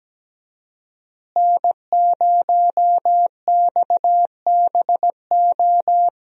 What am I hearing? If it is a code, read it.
Morse code, heard as a keyed tone, about 17 wpm, 700 Hz. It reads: N0XBO
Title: USA Call Signs - 5 Characters - Repeated 2x - 17wpm